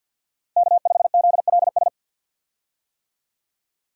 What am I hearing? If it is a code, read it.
Morse code, heard as a keyed tone, about 53 wpm, 700 Hz. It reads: K5ZLS
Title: USA Call Signs - 5 Characters - Repeated 3x - 50wpm